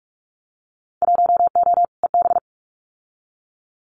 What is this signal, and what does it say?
Morse code, heard as a keyed tone, about 44 wpm, 700 Hz. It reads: 1O EB
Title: Winter Field Day Exchanges 45wpm